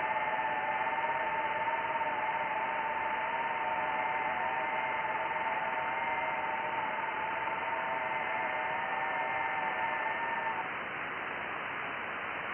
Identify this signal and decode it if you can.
FT8.
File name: IC-7100